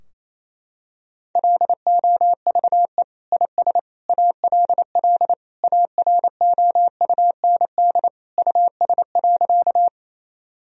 Morse code, heard as a keyed tone, about 28 wpm, 700 Hz. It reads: LOVE IS ALL AROUND US.
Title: Sentences from Top 1000 Words - Review Part 3 - 28wpm